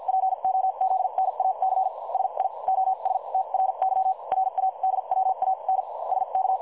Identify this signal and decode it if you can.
Morse code, heard as a keyed tone, about 50 wpm, spaced as for 40 wpm, 760 Hz. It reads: POZNAC SIOSTRY RADWAN SK
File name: dlazaawansowanych / nagraniazpasma / websdr recording 3565.0kHz